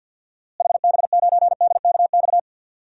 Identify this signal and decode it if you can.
Morse code, heard as a keyed tone, about 52 wpm, 700 Hz. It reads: HB9DKX
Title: DXCC-100 Call Signs - 6 Characters - Repeated 3x - 50wpm